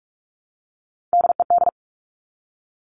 Morse code, heard as a keyed tone, about 46 wpm, 700 Hz. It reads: DED
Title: Top Three Letter Combinations in Words 45wpm